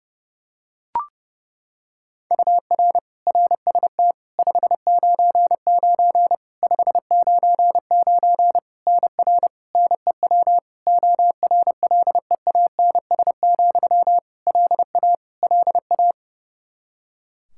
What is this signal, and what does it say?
Morse code, heard as a keyed tone, about 30 wpm, 700 Hz. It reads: UR RST 599 599 NR NEW ORLEANS, LA LA
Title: Practice QSO - Exchange 1 - Beginning - 30wpm